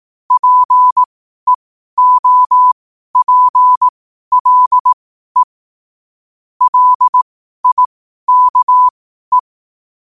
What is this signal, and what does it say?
Morse code, heard as a keyed tone, about 18 wpm, spaced as for 12 wpm, 1000 Hz. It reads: PEOPLE LIKE
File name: qotdfiles / 12wpm